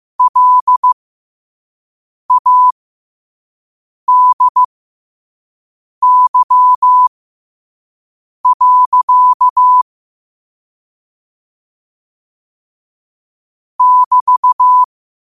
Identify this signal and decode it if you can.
Morse code, heard as a keyed tone, about 15 wpm, spaced as for 5 wpm, 1000 Hz. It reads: LADY. <BT>
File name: qotdfiles / 5wpm